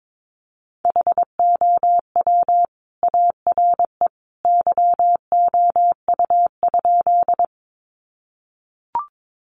Morse code, heard as a keyed tone, about 22 wpm, 700 Hz. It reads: HOW ARE YOU?
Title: Sentences from Top 100 Words 22wpm